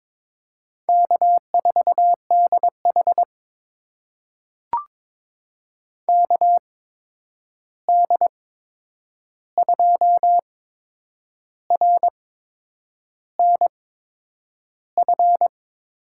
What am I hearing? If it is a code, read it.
Morse code, heard as a keyed tone, about 22 wpm, spaced as for 8 wpm, 700 Hz. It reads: K4DH KD2RNF
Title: USA Call Signs - 8x Character Spacing - 22wpm